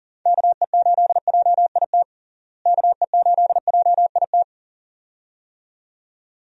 Morse code, heard as a keyed tone, about 40 wpm, 700 Hz. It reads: KE8JIT KE8JIT
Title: USA Call Signs - 6 Characters - Repeated 2x - 40wpm